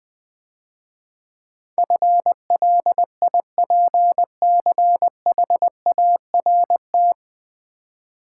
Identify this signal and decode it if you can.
Morse code, heard as a keyed tone, about 20 wpm, 700 Hz. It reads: FLIPCHART
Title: Easy Words - Speed-Racing with Hints - 20wpm